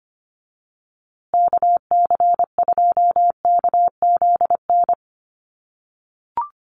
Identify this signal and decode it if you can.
Morse code, heard as a keyed tone, about 25 wpm, 700 Hz. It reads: KC2KZN
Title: USA Call Signs Repeated 3x 25wpm